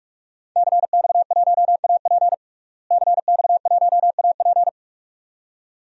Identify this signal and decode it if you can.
Morse code, heard as a keyed tone, about 47 wpm, 700 Hz. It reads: CX1AP CX1AP
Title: DXCC-100 Call Signs - 5 Characters - Repeated 2x - 45wpm